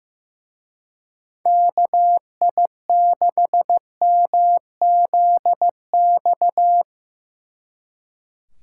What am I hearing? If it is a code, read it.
Morse code, heard as a keyed tone, about 17 wpm, 700 Hz. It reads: KI6MZX
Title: Speed Racing - USA 2x3 Call Signs 15wpm